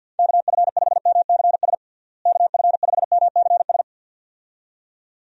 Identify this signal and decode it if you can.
Morse code, heard as a keyed tone, about 51 wpm, 700 Hz. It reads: KF5MCS KF5MCS
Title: USA Call Signs - 6 Characters - Repeated 2x - 50wpm